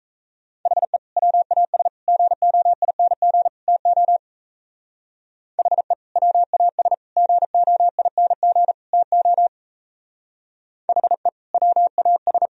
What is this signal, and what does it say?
Morse code, heard as a keyed tone, about 34 wpm, 700 Hz. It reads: HE WAS GOING TO HE WAS GOING TO HE WAS
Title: Speed-Racing - Sets of 4 Words 28wpm